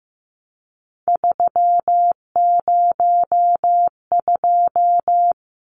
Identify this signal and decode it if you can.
Morse code, heard as a keyed tone, about 15 wpm, 700 Hz. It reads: 302
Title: Numbers - 3-Digit 15wpm